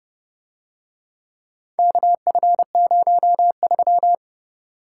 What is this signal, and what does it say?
Morse code, heard as a keyed tone, about 29 wpm, 700 Hz. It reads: KF03